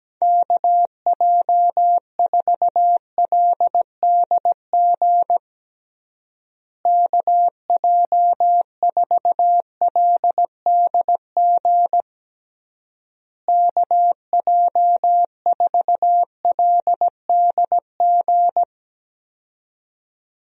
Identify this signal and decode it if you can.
Morse code, heard as a keyed tone, about 17 wpm, 700 Hz. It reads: KJ4LDG KJ4LDG KJ4LDG
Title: USA Call Signs - 6 Characters - Repeated 3x - 17wpm